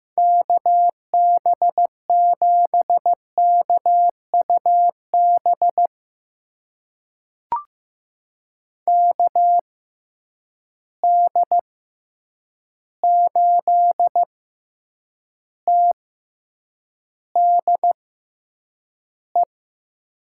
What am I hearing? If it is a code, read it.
Morse code, heard as a keyed tone, about 15 wpm, spaced as for 8 wpm, 700 Hz. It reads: KB7KUB KD8TDE
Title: USA Call Signs - 6 Characters - 6x Character Spacing - 15wpm